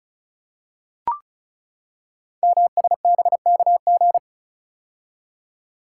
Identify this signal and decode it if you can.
Morse code, heard as a keyed tone, about 36 wpm, 690 Hz. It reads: MSBKG
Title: Code Groups - 5 Letters - 35wpm